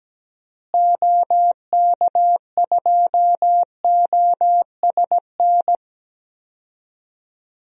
Morse code, heard as a keyed tone, about 17 wpm, 700 Hz. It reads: OK2OSN
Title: DXCC-100 Call Signs - 6 Characters - Repeated 3x - 17wpm